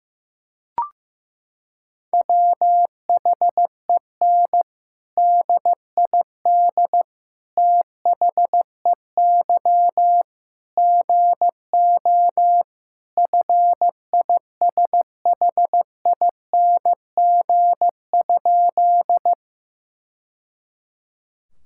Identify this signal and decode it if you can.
Morse code, heard as a keyed tone, about 15 wpm, 700 Hz. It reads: WHEN DID THEY GO FISHING?
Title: Sentences from Top 1000 Words - Review Part 5 - 15wpm